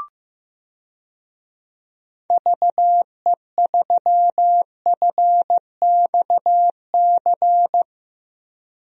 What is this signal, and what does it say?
Morse code, heard as a keyed tone, about 15 wpm, 700 Hz. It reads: VE3FXC
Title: DXCC-100 Call Signs - 6 Characters - Repeated 3x - 15wpm